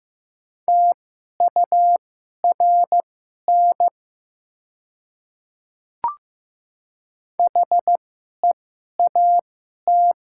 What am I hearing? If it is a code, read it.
Morse code, heard as a keyed tone, about 15 wpm, spaced as for 11 wpm, 700 Hz. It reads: TURN HEAT
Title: 2x Character Spacing - Top 500 Words - 15wpm